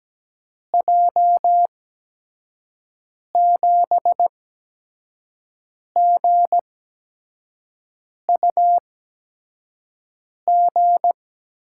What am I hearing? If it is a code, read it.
Morse code, heard as a keyed tone, about 17 wpm, spaced as for 5 wpm, 700 Hz. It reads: J7GUG
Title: USA Call Signs - 6 Characters - 8x Character Spacing - 17wpm